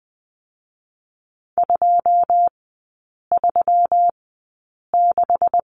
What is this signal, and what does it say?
Morse code, heard as a keyed tone, about 20 wpm, spaced as for 8 wpm, 700 Hz. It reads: 236
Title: 127 Lesson 21e - Number 6 - Three Characters 20wpm